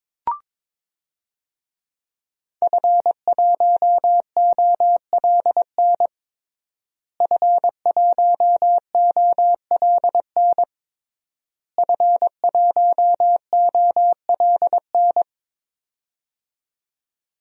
Morse code, heard as a keyed tone, about 22 wpm, 700 Hz. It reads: F1OLN F1OLN F1OLN
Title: DXCC-100 Call Signs - 5 Characters - Repeated 3x - 22wpm